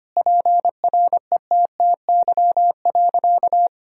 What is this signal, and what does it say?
Morse code, heard as a keyed tone, about 25 wpm, 700 Hz. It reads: PRETTY.
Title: Sentences from Top 1000 Words - Review Part 1 - 25wpm